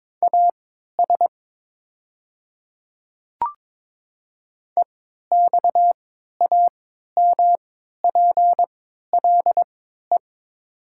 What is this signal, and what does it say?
Morse code, heard as a keyed tone, about 22 wpm, spaced as for 12 wpm, 700 Hz. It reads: AS EXAMPLE
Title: 3x Character Spacing - Top 300 Words - 22wpm